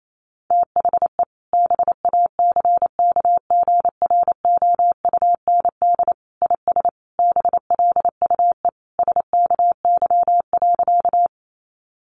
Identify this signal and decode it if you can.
Morse code, heard as a keyed tone, about 28 wpm, 700 Hz. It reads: THE BACKGROUND IS BLUE SKY.
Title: Sentences from Top 1300 Words 28wpm